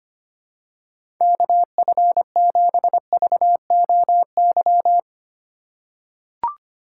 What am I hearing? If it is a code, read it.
Morse code, heard as a keyed tone, about 25 wpm, 700 Hz. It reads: KF7VOY
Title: USA Call Signs - 6 Characters - 4x Character Spacing - 25wpm